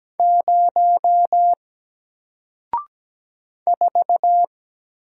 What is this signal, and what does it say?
Morse code, heard as a keyed tone, about 17 wpm, 700 Hz. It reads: 0 4
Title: Single Letter-Number Alt 17wpm